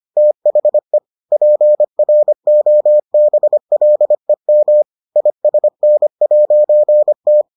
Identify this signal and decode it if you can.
Morse code, heard as a keyed tone, about 25 wpm, 600 Hz. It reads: THE PROBLEM ISN'T